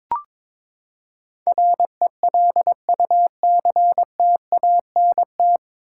Morse code, heard as a keyed tone, about 22 wpm, 700 Hz. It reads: RELUCTANT
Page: Morse Code Ninja practice 22wpm